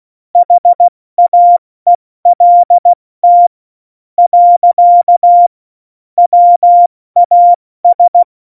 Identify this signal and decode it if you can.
Morse code, heard as a keyed tone, about 16 wpm, spaced as for 13 wpm, 700 Hz. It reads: HAELT. WAS
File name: Morsekurs / Witze / Witz 16. Januar 2018, 70 F80